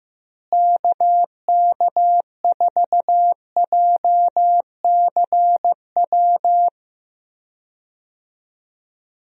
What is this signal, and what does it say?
Morse code, heard as a keyed tone, about 16 wpm, 700 Hz. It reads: KK4JCW
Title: USA Call Signs - 6 Characters - Repeated 2x - 15wpm